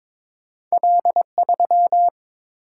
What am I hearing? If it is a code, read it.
Morse code, heard as a keyed tone, about 30 wpm, spaced as for 26 wpm, 700 Hz. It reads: L3
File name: Speed-Racing - Two Letter-Number Combinations - 22wpm